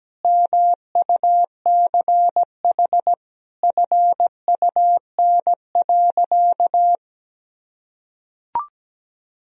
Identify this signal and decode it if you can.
Morse code, heard as a keyed tone, about 17 wpm, 700 Hz. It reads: MUCH FUN.
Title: Sentences from Top 1000 Words - Review Part 4 - 17wpm